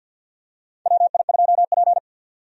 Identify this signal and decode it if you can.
Morse code, heard as a keyed tone, about 49 wpm, 700 Hz. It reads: WI2P
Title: USA Call Signs - 4 Characters - 50wpm